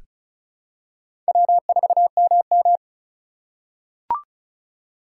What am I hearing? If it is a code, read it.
Morse code, heard as a keyed tone, about 36 wpm, 700 Hz. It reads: W4MM
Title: USA Call Signs - 2x Character Spacing - 35wpm